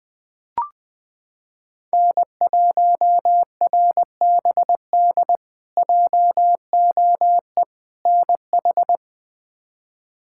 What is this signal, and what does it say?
Morse code, heard as a keyed tone, about 20 wpm, 700 Hz. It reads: N1RBD JOE NH
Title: K1USN SST QSO Practice - 20wpm